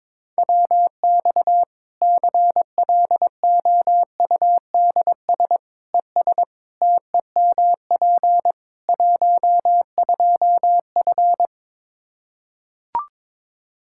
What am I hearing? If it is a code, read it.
Morse code, heard as a keyed tone, about 22 wpm, 700 Hz. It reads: WX CLOUDS ES TEMP 12F
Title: K1USN SST QSO Practice 22wpm